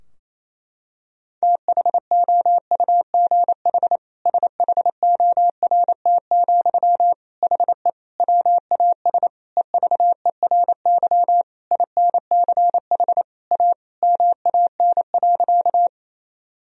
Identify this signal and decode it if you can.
Morse code, heard as a keyed tone, about 28 wpm, 700 Hz. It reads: THOUGH SHORT, HE WAS EVERY INCH A MAN.
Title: Sentences from Top 1000 Words - Review Part 3 - 28wpm